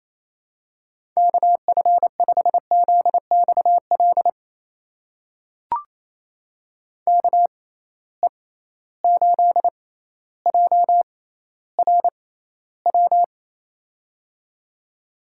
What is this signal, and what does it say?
Morse code, heard as a keyed tone, about 28 wpm, spaced as for 14 wpm, 700 Hz. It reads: KF5ZXL KE8JRW
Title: USA Call Signs - 6 Characters - 6x Character Spacing - 28wpm